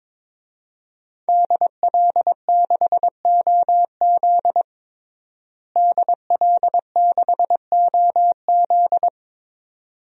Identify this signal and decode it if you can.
Morse code, heard as a keyed tone, about 22 wpm, 700 Hz. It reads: DL6OZ DL6OZ
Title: DXCC-100 Call Signs - 5 Characters - Repeated 2x - 22wpm